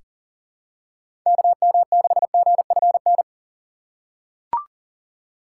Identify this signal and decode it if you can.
Morse code, heard as a keyed tone, about 40 wpm, 700 Hz. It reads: KM6GFN
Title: USA Call Signs - 6 Characters - 4x Character Spacing - 40wpm